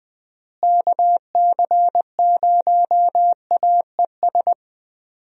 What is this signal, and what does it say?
Morse code, heard as a keyed tone, about 20 wpm, 700 Hz. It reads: KC0AES
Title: USA Call Signs - 6 Characters - Repeated 3x - 20wpm